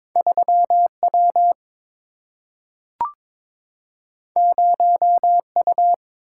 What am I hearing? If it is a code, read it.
Morse code, heard as a keyed tone, about 22 wpm, 700 Hz. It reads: 3W 0U
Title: Single-Letter Single-Number 22wpm